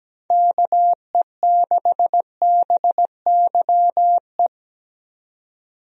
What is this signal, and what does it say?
Morse code, heard as a keyed tone, about 17 wpm, 700 Hz. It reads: KE6BYE
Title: USA Call Signs - 6 Characters - 17wpm